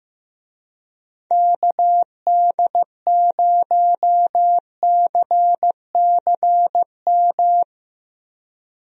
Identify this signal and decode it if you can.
Morse code, heard as a keyed tone, about 15 wpm, 700 Hz. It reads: KD0CCM
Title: USA Call Signs - 6 Characters - Repeated 2x - 15wpm